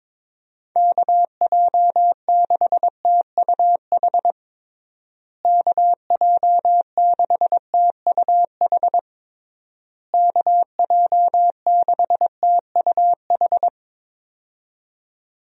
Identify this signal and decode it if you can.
Morse code, heard as a keyed tone, about 22 wpm, 700 Hz. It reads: KJ6TUH KJ6TUH KJ6TUH
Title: USA Call Signs - 6 Characters - Repeated 3x - 22wpm